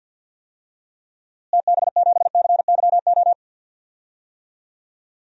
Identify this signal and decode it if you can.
Morse code, heard as a keyed tone, about 53 wpm, 700 Hz. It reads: TB7CYO
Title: DXCC-100 Call Signs - 6 Characters - Repeated 3x - 50wpm